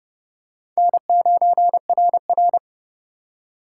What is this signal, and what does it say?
Morse code, heard as a keyed tone, about 29 wpm, 700 Hz. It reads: N9RR